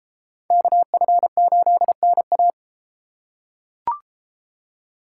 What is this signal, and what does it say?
Morse code, heard as a keyed tone, about 32 wpm, 700 Hz. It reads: KF8NA